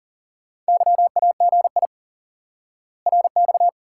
Morse code, heard as a keyed tone, about 40 wpm, 700 Hz. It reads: YAGI RX
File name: IQE Course 3 - Lesson 89 - YAGI - Two Elements - 40wpm